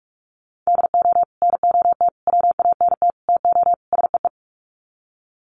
Morse code, heard as a keyed tone, about 44 wpm, 700 Hz. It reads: DO NOT WANT TO SEE
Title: Sets of 5 Words 45wpm